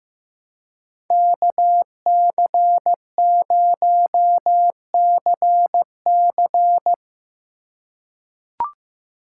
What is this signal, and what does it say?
Morse code, heard as a keyed tone, about 15 wpm, 700 Hz. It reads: KC0CC
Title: USA Call Signs - 5 Characters - 3x Character Spacing - 15wpm